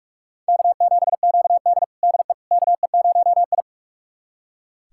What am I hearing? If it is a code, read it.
Morse code, heard as a keyed tone, about 44 wpm, 700 Hz. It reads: K7QD DE KE0I
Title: USA Callsign DE Callsign - 4 Characters - 45wpm